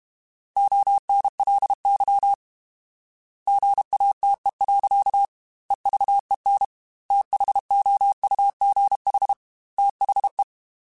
Morse code, heard as a keyed tone, about 32 wpm, 800 Hz. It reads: ONLY GATE. EVEN THOUGH THE
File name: Technical Subjects / Technical Subjects-mp3s / AND Gate @30